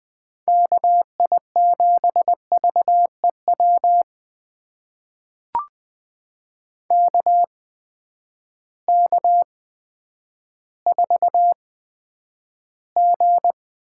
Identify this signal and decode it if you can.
Morse code, heard as a keyed tone, about 20 wpm, spaced as for 10 wpm, 700 Hz. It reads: KI7VEW KK4G
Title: USA Call Signs - 6 Characters - 8x Character Spacing - 20wpm